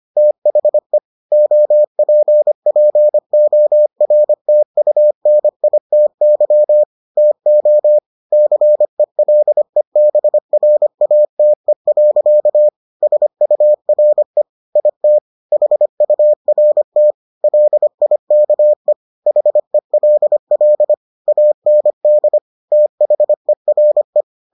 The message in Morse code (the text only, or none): THE OPPORTUNITY TO CELEBRATE. SURE IT HURT LIKE HELL AND THERE